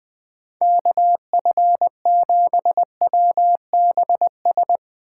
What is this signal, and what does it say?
Morse code, heard as a keyed tone, about 22 wpm, 700 Hz. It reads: KF7WBS
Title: Speed Racing - USA 2x3 Call Signs 20wpm